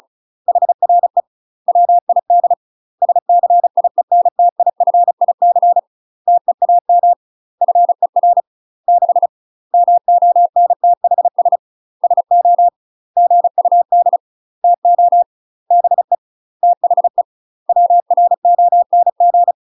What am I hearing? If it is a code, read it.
Morse code, heard as a keyed tone, about 36 wpm, 700 Hz. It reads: HRE WID SCIENTIFIC TEAM FER 6 MONTHS SO GUD TO BE THE WRONG
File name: UK MASTER QSOs TRAINING / 35WPM / UK MASTER QSOs 0016